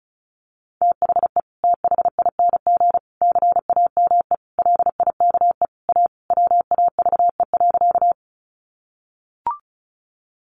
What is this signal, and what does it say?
Morse code, heard as a keyed tone, about 34 wpm, 700 Hz. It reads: THE THING CAME LIKE A WAVE.
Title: Sentences from Top 1400 Words 35wpm